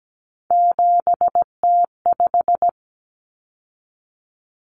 Morse code, heard as a keyed tone, about 17 wpm, 700 Hz. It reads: 7T5
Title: Common Cut Numbers - 3-Digit 17wpm